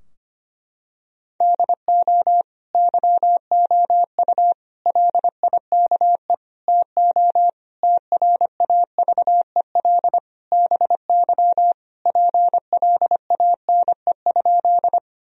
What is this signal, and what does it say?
Morse code, heard as a keyed tone, about 25 wpm, 700 Hz. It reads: DO YOU LIKE TO TRAVEL BY PLANE?
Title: Sentences from Top 1000 Words - Review Part 3 - 25wpm